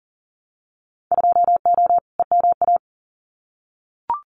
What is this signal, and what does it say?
Morse code, heard as a keyed tone, about 40 wpm, 700 Hz. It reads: 2O EMA
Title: Winter Field Day Exchanges Encore 40wpm